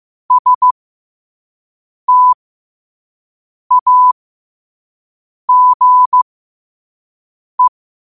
Morse code, heard as a keyed tone, about 15 wpm, spaced as for 5 wpm, 1000 Hz. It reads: STAGE
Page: New quotes every day in morse code at 5 Words per minute.